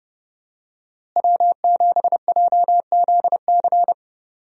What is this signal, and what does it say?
Morse code, heard as a keyed tone, about 32 wpm, 700 Hz. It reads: W7JZC
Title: USA Call Signs - 5 Characters - Repeated 2x - 30wpm